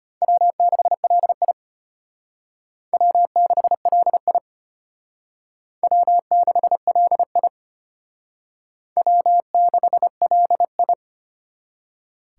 Morse code, heard as a keyed tone, about 32 wpm, 700 Hz. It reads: W6LI W6LI W6LI W6LI